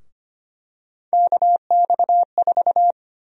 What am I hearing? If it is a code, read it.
Morse code, heard as a keyed tone, about 25 wpm, 700 Hz. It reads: KX4
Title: USA Call Sign Prefixes - 25wpm